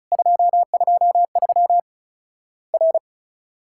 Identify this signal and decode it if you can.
Morse code, heard as a keyed tone, about 35 wpm, 700 Hz. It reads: 223 R
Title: Worked All Europe DX Contest - QTC Exchange - 35wpm